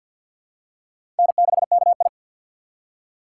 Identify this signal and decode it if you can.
Morse code, heard as a keyed tone, about 47 wpm, 700 Hz. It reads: N6KI
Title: USA Call Signs - 4 Characters - 2x Character Spacing - 50wpm